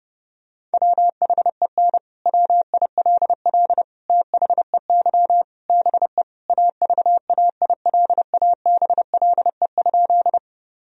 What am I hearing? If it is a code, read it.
Morse code, heard as a keyed tone, about 32 wpm, 700 Hz. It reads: WHEN WILL THEY BE AVAILABLE?
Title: Sentences from Top 1000 Words - Review Part 3 - 30wpm